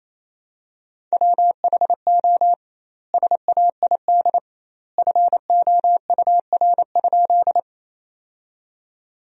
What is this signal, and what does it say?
Morse code, heard as a keyed tone, about 28 wpm, 700 Hz. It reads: WHO SAID FOUR?
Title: 2x Word-Spacing - Sentences from Top 300 Words - 28wpm